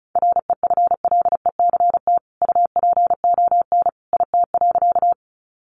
Morse code, heard as a keyed tone, about 34 wpm, 700 Hz. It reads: REFLECT UPON IT.
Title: Sentences from Top 1500 Words 35wpm